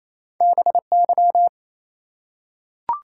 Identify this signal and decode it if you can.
Morse code, heard as a keyed tone, about 29 wpm, 700 Hz. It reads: BY
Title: Top 180 QSO Elements 28wpm